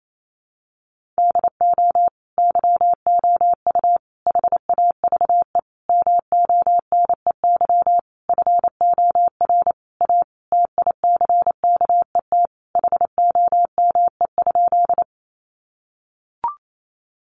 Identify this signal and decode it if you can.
Morse code, heard as a keyed tone, about 28 wpm, 700 Hz. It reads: DO YOU HAVE MONEY FOR A TICKET HOME?
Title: Sentences from Top 1400 Words 28wpm